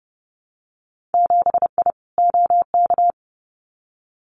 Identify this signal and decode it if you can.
Morse code, heard as a keyed tone, about 31 wpm, 700 Hz. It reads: 7I OK
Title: Winter Field Day Exchanges Encore 30wpm